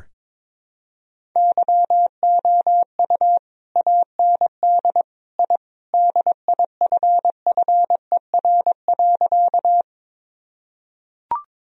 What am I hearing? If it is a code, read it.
Morse code, heard as a keyed tone, about 22 wpm, 700 Hz. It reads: YOU AND I DIFFER.
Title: Sentences from Top 1000 Words - Review Part 1 - 22wpm